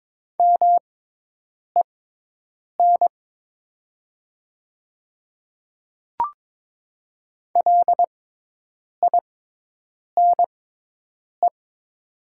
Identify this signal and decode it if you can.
Morse code, heard as a keyed tone, about 22 wpm, spaced as for 8 wpm, 700 Hz. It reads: MEN LINE
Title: 6x Character Spacing - Top 200 Words - 22wpm